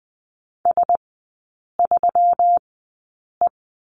Morse code, heard as a keyed tone, about 20 wpm, spaced as for 8 wpm, 700 Hz. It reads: S3E
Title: 121 Lesson 20e - Number 3 - Three Characters 20wpm